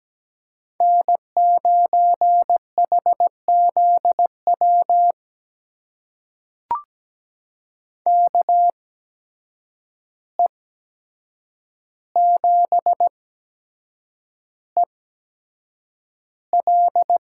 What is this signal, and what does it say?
Morse code, heard as a keyed tone, about 17 wpm, spaced as for 7 wpm, 700 Hz. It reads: N9HZW KE7EL
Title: USA Call Signs - 5 Characters - 8x Character Spacing - 17wpm